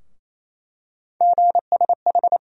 Morse code, heard as a keyed tone, about 28 wpm, 700 Hz. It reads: GSH
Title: USA Call Sign Postfixes - 3 Letters 28wpm